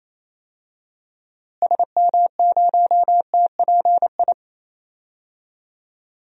Morse code, heard as a keyed tone, about 28 wpm, 700 Hz. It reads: SM0TPI
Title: DXCC-100 Call Signs - 6 Characters - Repeated 2x - 28wpm